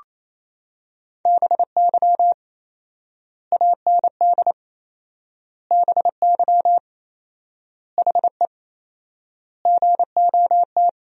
Morse code, heard as a keyed tone, about 28 wpm, 700 Hz. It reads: BY AND BY HE GOT
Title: 4x Word-Spacing - Sentences from Top 300 Words - 28wpm